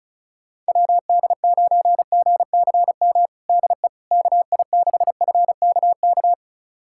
Morse code, heard as a keyed tone, about 36 wpm, 700 Hz. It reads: WD9GCM DE KI6FKK
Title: USA Callsign DE Callsign - 6 Characters - 35wpm